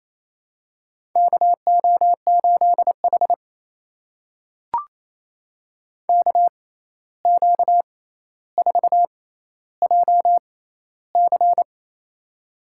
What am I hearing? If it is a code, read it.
Morse code, heard as a keyed tone, about 28 wpm, spaced as for 13 wpm, 700 Hz. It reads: KO8H KQ4JC
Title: USA Call Signs - 6x Character Spacing - 28wpm